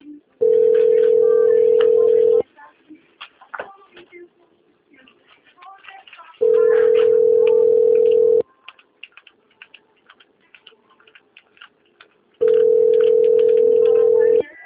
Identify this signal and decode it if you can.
Ringback tone (North America), 440 + 480 Hz, with speech.
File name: out- - - - - .5178